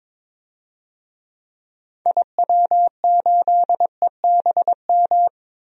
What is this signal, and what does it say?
Morse code, heard as a keyed tone, about 22 wpm, 700 Hz. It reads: IW8EBM
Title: DXCC-100 Call Signs - 6 Characters - 22wpm